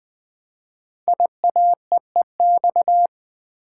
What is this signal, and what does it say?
Morse code, heard as a keyed tone, about 20 wpm, 700 Hz. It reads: IAEEX
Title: Code Groups - 5 Letter:Numbers - 20wpm